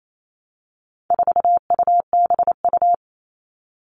Morse code, heard as a keyed tone, about 28 wpm, 700 Hz. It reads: 4UBU
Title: All Cut Numbers - 4-Digit 28wpm